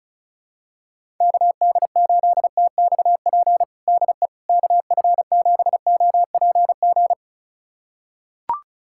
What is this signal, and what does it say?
Morse code, heard as a keyed tone, about 36 wpm, 700 Hz. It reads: KD8TXP DE KF7OPG
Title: USA Callsign DE Callsign - 6 Characters - 35wpm